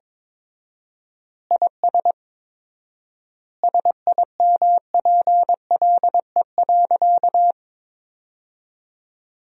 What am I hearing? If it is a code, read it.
Morse code, heard as a keyed tone, about 22 wpm, 700 Hz. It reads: IS SIMPLE.
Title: 4x Word-Spacing - Sentences from Top 400 Words - 22wpm